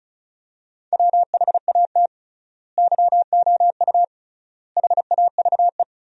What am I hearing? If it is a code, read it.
Morse code, heard as a keyed tone, about 34 wpm, 700 Hz. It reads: WHAT YOU HAVE
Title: 3x Word-Spacing - Sentences from Top 200 Words - 35wpm